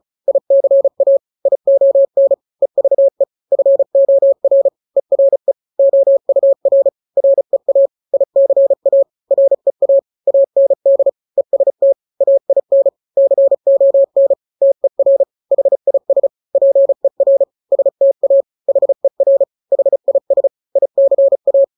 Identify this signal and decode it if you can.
Morse code, heard as a keyed tone, about 34 wpm, 550 Hz. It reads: ICA ION EVE FOR ERE OUR REA ICA REA AND EST AIN CON TER HIS PER STA HER HIS ICA
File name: 3-grams 35 30 WPM